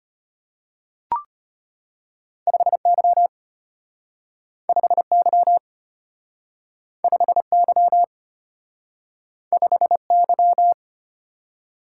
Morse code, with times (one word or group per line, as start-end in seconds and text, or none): 2.5-3.3 5Y
4.7-5.6 5Y
7.0-8.0 5Y
9.5-10.7 5Y